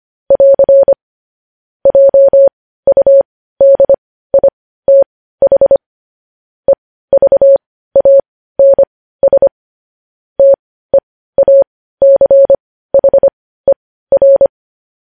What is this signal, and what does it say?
Morse code, heard as a keyed tone, about 25 wpm, spaced as for 15 wpm, 550 Hz. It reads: <AR> JUDITH EVANS TEACHER
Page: news-headlines-in-morse-code-at-15-wpm